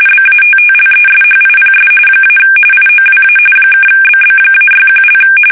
RTTY, about 45.45 baud. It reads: BATTLESHIPS - IOWA (BB-61)
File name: fsk850 / battleshipnj